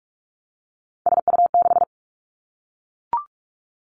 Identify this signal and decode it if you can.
Morse code, heard as a keyed tone, about 46 wpm, 700 Hz. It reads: SU6
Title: Three Challenging Letter-Number Combinations 45wpm